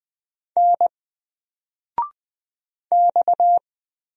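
Morse code, heard as a keyed tone, about 20 wpm, 700 Hz. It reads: N X
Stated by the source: Single letter and number Morse drill at 20 wpm, from N4LCD.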